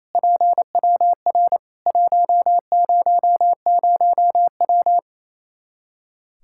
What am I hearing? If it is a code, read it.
Morse code, heard as a keyed tone, about 27 wpm, 700 Hz. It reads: PWR 100W
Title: Practice QSO - Exchange 2 - Info - 28wpm